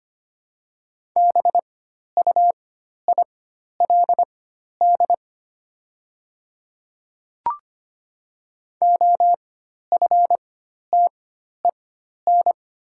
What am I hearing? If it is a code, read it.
Morse code, heard as a keyed tone, about 25 wpm, spaced as for 12 wpm, 700 Hz. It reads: BUILD OFTEN
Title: 4x Character Spacing - Top 300 Words - 25wpm